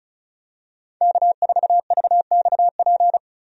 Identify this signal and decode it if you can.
Morse code, heard as a keyed tone, about 36 wpm, 700 Hz. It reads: K4VXP
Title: USA Call Signs - 5 Characters - 35wpm